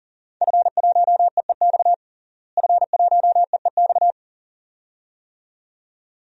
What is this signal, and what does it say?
Morse code, heard as a keyed tone, about 37 wpm, 700 Hz. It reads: F1EEX F1EEX
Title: DXCC-100 Call Signs - 5 Characters - Repeated 2x - 40wpm